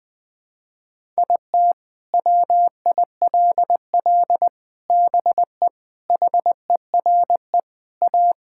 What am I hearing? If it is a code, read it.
Morse code, heard as a keyed tone, about 20 wpm, 700 Hz. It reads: IT WILL BE HERE A
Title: Sentences from Top 1000 Words - Review Part 5 - 20wpm